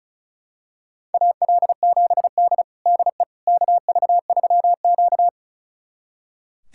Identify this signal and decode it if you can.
Morse code, heard as a keyed tone, about 34 wpm, 700 Hz. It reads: AL7D DE KV3Q
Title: USA Callsign DE Callsign - 4 Characters - 35wpm